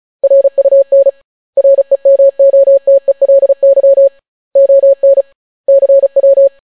Morse code, heard as a keyed tone, about 36 wpm, 550 Hz. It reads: RUN REMOTELY ON CW